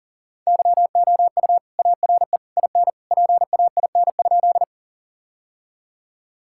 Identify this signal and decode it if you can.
Morse code, heard as a keyed tone, about 40 wpm, 700 Hz. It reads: YOU ARE IN PAIN?
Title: Sentences from Top 1600 Words 40wpm